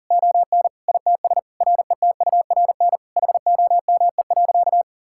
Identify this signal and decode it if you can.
Morse code, heard as a keyed tone, about 40 wpm, 700 Hz. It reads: ON ITS RETURN HOME.
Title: Sentences from Top 1000 Words - Review Part 5 - 40wpm